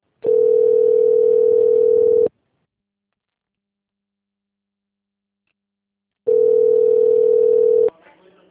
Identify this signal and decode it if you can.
Ringback tone (North America), 440 + 480 Hz.
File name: rg-1600-unknown- - - .11679